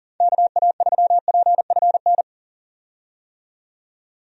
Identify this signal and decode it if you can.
Morse code, heard as a keyed tone, about 38 wpm, 700 Hz. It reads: KA3PFN
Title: USA Call Signs - 6 Characters - 40wpm